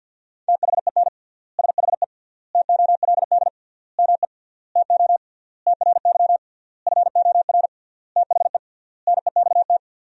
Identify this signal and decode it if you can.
Morse code, heard as a keyed tone, about 50 wpm, 700 Hz. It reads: THEN SHE TOLD ME TO TRY FOR THE NEXT
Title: 3x Word-Spacing - Sentences from Top 400 Words - 50wpm